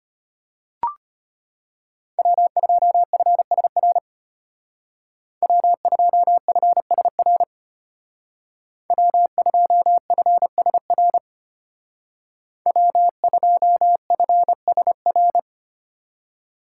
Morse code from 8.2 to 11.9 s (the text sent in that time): W2FSR